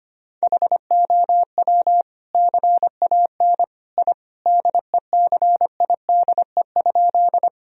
Morse code, heard as a keyed tone, about 25 wpm, 700 Hz. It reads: HOW CAN I DECIDE?
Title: Sentences from Top 1000 Words - Review Part 3 - 25wpm